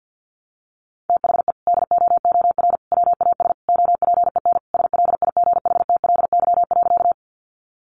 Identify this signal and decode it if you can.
Morse code, heard as a keyed tone, about 50 wpm, 700 Hz. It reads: THE DOOR WAS OPEN SLIGHTLY.
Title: Sentences from Top 1300 Words 50wpm